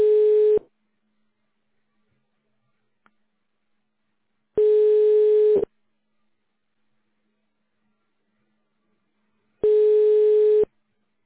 Ringback tone (Europe, 425 Hz).